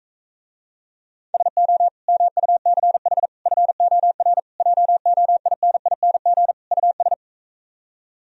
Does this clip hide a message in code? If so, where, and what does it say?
Yes: Morse code from 1.3 to 7.1 s, heard as a keyed tone, about 45 wpm, 700 Hz. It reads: SO MUCH FOR JOINING US